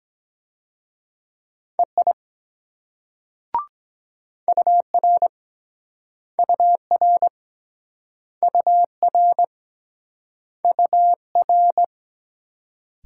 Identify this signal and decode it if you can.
Morse code, heard as a keyed tone, about 23 wpm, 700 Hz. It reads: EI UR UR UR UR